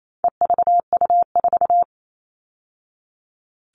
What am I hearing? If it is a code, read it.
Morse code, heard as a keyed tone, about 28 wpm, 700 Hz. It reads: EVU4
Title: All Cut Numbers - 4-Digit 28wpm